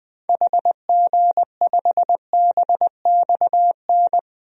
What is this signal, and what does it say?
Morse code, heard as a keyed tone, about 20 wpm, 700 Hz. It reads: HG5BXN
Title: DXCC-100 Call Signs - 6 Characters - 20wpm